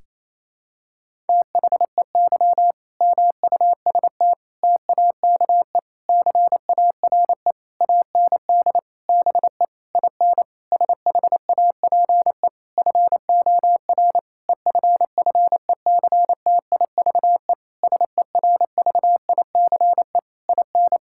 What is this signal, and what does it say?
Morse code, heard as a keyed tone, about 28 wpm, 700 Hz. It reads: THEY MUST TAKE CARE AND BE IN SHAPE FOR EFFECTIVE SERVICE IN
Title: Sentences from Top 1000 Words - Review Part 6 - 28wpm